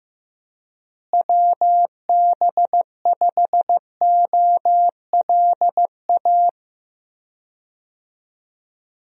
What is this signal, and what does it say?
Morse code, heard as a keyed tone, about 15 wpm, 700 Hz. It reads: WB5OLA
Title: USA Call Signs - 6 Characters - Repeated 2x - 15wpm